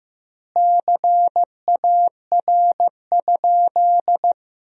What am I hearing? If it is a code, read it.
Morse code, heard as a keyed tone, about 15 wpm, 700 Hz. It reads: CAR?
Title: 2x Word-Spacing - Sentences from Top 300 Words - 15wpm